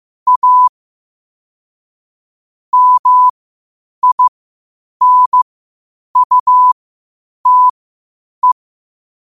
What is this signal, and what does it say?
Morse code, heard as a keyed tone, about 15 wpm, spaced as for 8 wpm, 1000 Hz. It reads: A MINUTE